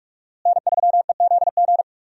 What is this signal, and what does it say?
Morse code, heard as a keyed tone, about 44 wpm, 700 Hz. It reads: N3EZG
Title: USA Call Signs - 5 Characters - Repeated 3x - 45wpm